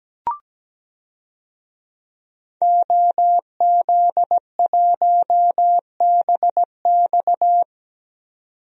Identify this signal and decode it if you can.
Morse code, heard as a keyed tone, about 17 wpm, 700 Hz. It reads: OZ1BX
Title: DXCC-100 Call Signs - 5 Characters - 17wpm